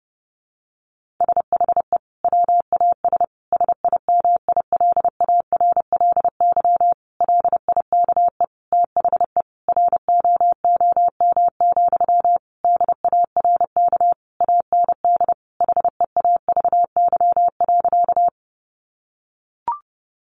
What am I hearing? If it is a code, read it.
Morse code, heard as a keyed tone, about 30 wpm, 700 Hz. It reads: SHE WAS SIMILARLY LIKE THE ROOM, DARK AND HEAVY.
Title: Sentences from Top 1400 Words 30wpm